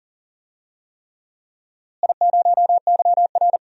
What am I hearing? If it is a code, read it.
Morse code, heard as a keyed tone, about 40 wpm, 700 Hz. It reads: I0YR
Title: DXCC-100 Call Signs - 4 Characters - Repeated 3x - 40wpm